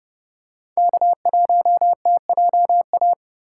Morse code, heard as a keyed tone, about 27 wpm, 700 Hz. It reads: K1TJA